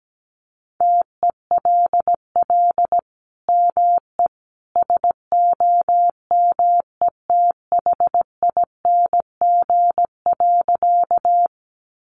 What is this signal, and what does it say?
Morse code, heard as a keyed tone, about 17 wpm, 700 Hz. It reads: TELL ME SOMETHING.
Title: Sentences from Top 600 Words 17wpm